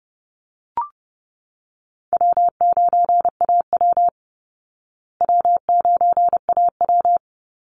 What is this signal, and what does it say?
Morse code, heard as a keyed tone, about 30 wpm, 700 Hz. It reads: W9AW W9AW
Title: Easy Call Signs Repeated 3x - 30wpm